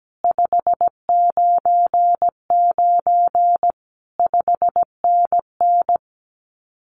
Morse code, heard as a keyed tone, about 17 wpm, 700 Hz. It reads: 599 5NN
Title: Common QSO Phrases Encore 17wpm